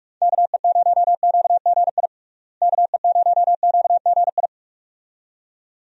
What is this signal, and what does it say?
Morse code, heard as a keyed tone, about 46 wpm, 700 Hz. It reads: KE0QGI KE0QGI
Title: USA Call Signs - 6 Characters - Repeated 2x - 45wpm